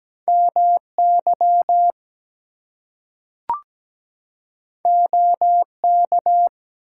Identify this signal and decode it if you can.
Morse code, heard as a keyed tone, about 17 wpm, 700 Hz. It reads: MY OK
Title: Two Letter Words - 17wpm